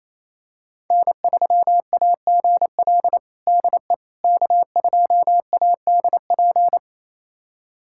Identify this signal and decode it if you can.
Morse code, heard as a keyed tone, about 28 wpm, 700 Hz. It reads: N3AGL DE K2ADP
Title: USA Callsign DE Callsign - 5 Characters - 28wpm